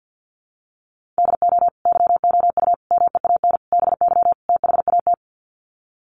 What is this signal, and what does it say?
Morse code, heard as a keyed tone, about 50 wpm, 700 Hz. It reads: DO YOU MEAN BY THAT
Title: Sets of 5 Words Encore2 50wpm